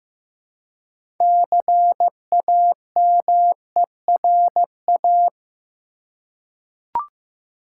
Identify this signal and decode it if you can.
Morse code, heard as a keyed tone, about 15 wpm, 700 Hz. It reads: CAMERA
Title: Six Letter Words - 15wpm